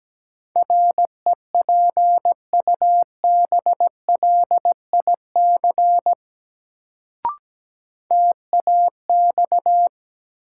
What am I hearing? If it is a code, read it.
Morse code, heard as a keyed tone, about 17 wpm, 700 Hz. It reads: REPUBLIC TAX
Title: Sentences from Top 1500 Words - New Word Drill Alt - 17wpm